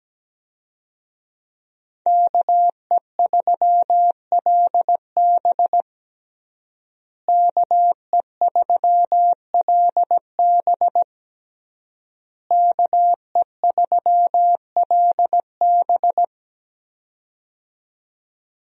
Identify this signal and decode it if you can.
Morse code, heard as a keyed tone, about 17 wpm, 700 Hz. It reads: KE3LB KE3LB KE3LB
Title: USA Call Signs - 5 Characters - Repeated 3x - 17wpm